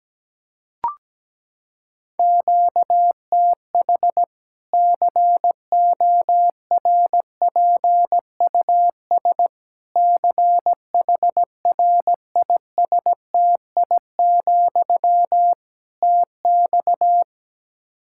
Morse code, heard as a keyed tone, about 17 wpm, 700 Hz. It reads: QTH CORPUS CHRISTI, TX
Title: K1USN SST QSO Practice 17wpm